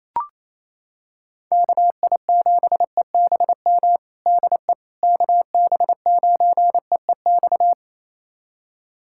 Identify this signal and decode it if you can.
Morse code, heard as a keyed tone, about 28 wpm, 700 Hz. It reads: KI7EBM DE KB9EEX
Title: USA Callsign DE Callsign - 6 Characters - 28wpm